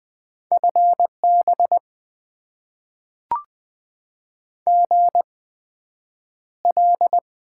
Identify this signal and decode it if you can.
Morse code, heard as a keyed tone, about 20 wpm, spaced as for 9 wpm, 700 Hz. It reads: FB GL
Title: Top 90 QSO Elements - 8x Character Spacing - 20wpm